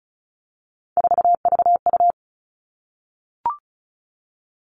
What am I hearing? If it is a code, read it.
Morse code, heard as a keyed tone, about 36 wpm, 700 Hz. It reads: 4VU
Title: Three Challenging Letter-Number Combinations 35wpm